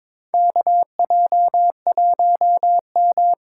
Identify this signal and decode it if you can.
Morse code, heard as a keyed tone, about 23 wpm, 700 Hz. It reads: KJ1M
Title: USA Call Signs - 4 Characters - 8x Character Spacing - 22wpm